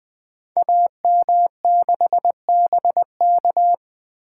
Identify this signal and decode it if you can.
Morse code, heard as a keyed tone, about 20 wpm, 700 Hz. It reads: AM6BK
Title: Code Groups - 5 Letter:Numbers - 20wpm